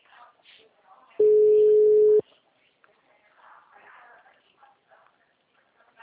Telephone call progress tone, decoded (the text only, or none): ringback tone